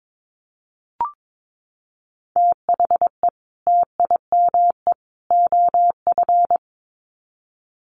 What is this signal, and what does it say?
Morse code, heard as a keyed tone, about 22 wpm, 700 Hz. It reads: THE TIME OF
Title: Sets of 3 Words 22wpm